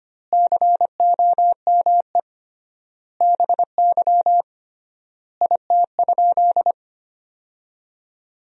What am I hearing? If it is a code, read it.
Morse code, heard as a keyed tone, about 25 wpm, 700 Hz. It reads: COME BY IT?
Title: 3x Word-Spacing - Sentences from Top 100 Words - 25wpm